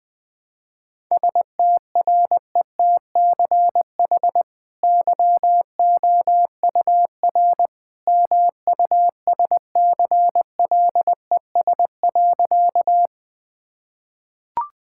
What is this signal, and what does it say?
Morse code, heard as a keyed tone, about 19 wpm, 700 Hz. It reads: STRETCH YOUR MUSCLES.
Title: Easy Sentences with Hints - 20wpm